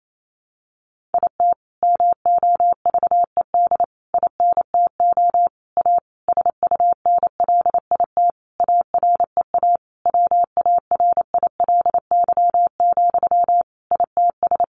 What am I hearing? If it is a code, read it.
Morse code, heard as a keyed tone, about 28 wpm, 700 Hz. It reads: IT MOVED INTO A SUNLIT AREA WARILY, ITS
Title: A Sound Of Thunder 28wpm